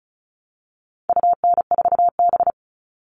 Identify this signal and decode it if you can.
Morse code, heard as a keyed tone, about 34 wpm, 700 Hz. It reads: UN4B
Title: All Cut Numbers - 4-Digit 35wpm